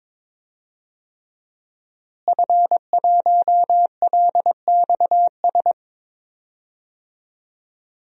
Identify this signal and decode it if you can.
Morse code, heard as a keyed tone, about 22 wpm, 700 Hz. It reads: F1LXS
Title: DXCC-100 Call Signs - 5 Characters - Repeated 3x - 22wpm